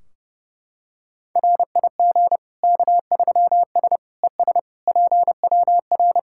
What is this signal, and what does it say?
Morse code, heard as a keyed tone, about 32 wpm, 700 Hz. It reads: RIG K3S ES PWR
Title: K1USN SST QSO Practice 30wpm